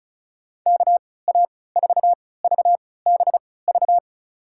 Morse code, heard as a keyed tone, about 34 wpm, spaced as for 20 wpm, 700 Hz. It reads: KA4VBV
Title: USA Call Signs - 3x Character Spacing - 35wpm